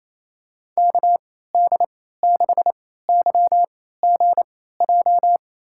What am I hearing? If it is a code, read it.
Morse code, heard as a keyed tone, about 28 wpm, spaced as for 16 wpm, 700 Hz. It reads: KD6YGJ
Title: USA Call Signs - 3x Character Spacing - 28wpm